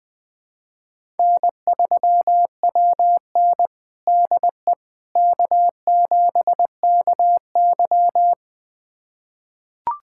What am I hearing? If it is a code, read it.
Morse code, heard as a keyed tone, about 20 wpm, 700 Hz. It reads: N3WN DE K7KY
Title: USA Callsign DE Callsign - 4 Characters - 20wpm